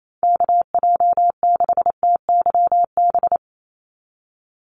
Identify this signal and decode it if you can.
Morse code, heard as a keyed tone, about 28 wpm, 700 Hz. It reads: KJ6TYB
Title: USA Call Signs 28wpm